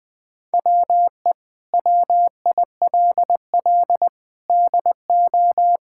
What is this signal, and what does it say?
Morse code, heard as a keyed tone, about 20 wpm, 700 Hz. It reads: WE WILL DO
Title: Sentences with 6-Word Phrases - Spoken Beginning - 20wpm